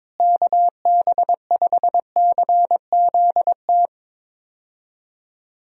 Morse code, heard as a keyed tone, about 22 wpm, 700 Hz. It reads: KB5CZT
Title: USA Call Signs - 6 Characters - Repeated 3x - 22wpm